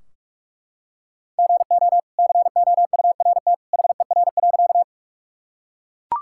Morse code, heard as a keyed tone, about 44 wpm, 700 Hz. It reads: GO COURT HER.
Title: Sentences from Top 1600 Words 45wpm